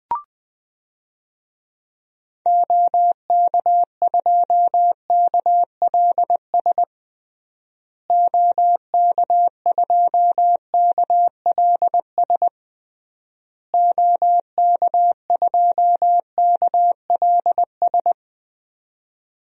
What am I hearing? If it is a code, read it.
Morse code, heard as a keyed tone, about 21 wpm, 700 Hz. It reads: OK2KLS OK2KLS OK2KLS
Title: DXCC-100 Call Signs - 6 Characters - Repeated 3x - 20wpm